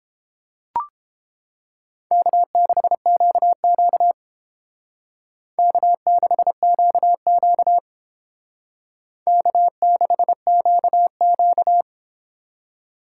Morse code, heard as a keyed tone, about 30 wpm, 700 Hz. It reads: K6QQ K6QQ K6QQ